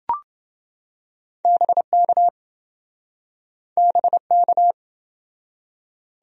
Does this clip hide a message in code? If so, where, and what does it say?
Yes: Morse code from 1.4 to 4.7 s, heard as a keyed tone, about 27 wpm, 700 Hz. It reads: BK BK